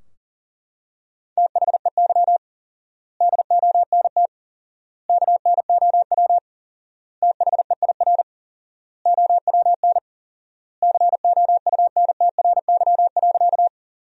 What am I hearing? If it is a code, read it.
Morse code, heard as a keyed tone, about 40 wpm, 700 Hz. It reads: THEY DONT KNOW THEIR OWN COUNTRY.
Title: 4x Word-Spacing - Sentences from Top 300 Words - 40wpm